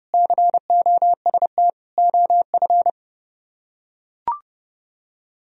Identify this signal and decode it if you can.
Morse code, heard as a keyed tone, about 31 wpm, 700 Hz. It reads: COST OF